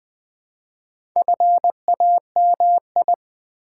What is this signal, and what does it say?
Morse code, heard as a keyed tone, about 20 wpm, 700 Hz. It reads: FAMI
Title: Build A Word - 2 Letters At A Time - 8-Letter Words - 20wpm